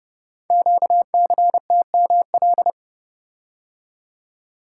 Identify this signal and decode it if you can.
Morse code, heard as a keyed tone, about 30 wpm, 700 Hz. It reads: QCTML
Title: Code Groups - 5 Letters - 30wpm